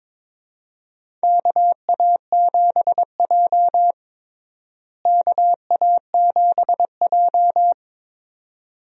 Morse code, heard as a keyed tone, about 23 wpm, 700 Hz. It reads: KA7J KA7J
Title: USA Call Signs - 4 Characters - Repeated 2x - 22wpm